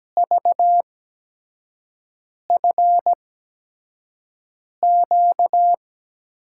Morse code, heard as a keyed tone, about 17 wpm, spaced as for 5 wpm, 700 Hz. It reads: VFQ
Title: USA Call Signs - 5 Characters - 8x Character Spacing - 17wpm